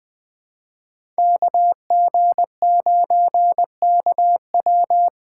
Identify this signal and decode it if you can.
Morse code, heard as a keyed tone, about 21 wpm, 700 Hz. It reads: KG9KW
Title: USA Call Signs - 5 Characters - 3x Character Spacing - 20wpm